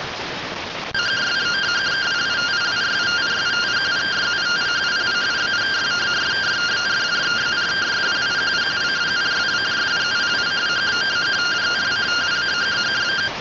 Two-tone teleprinter signal, about 170 Hz shift.